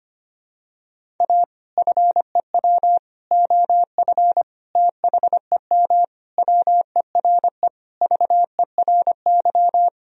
Morse code, heard as a keyed tone, about 25 wpm, 700 Hz. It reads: A FEW OF THEM WERE VERY
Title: Sentences from Top 1000 Words - Review Part 1 - 25wpm